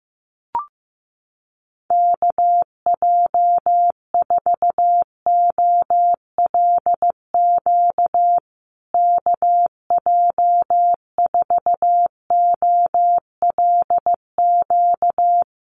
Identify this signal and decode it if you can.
Morse code, heard as a keyed tone, about 15 wpm, 700 Hz. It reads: KJ4OLQ KJ4OLQ
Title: USA Call Signs Repeated 2x 15wpm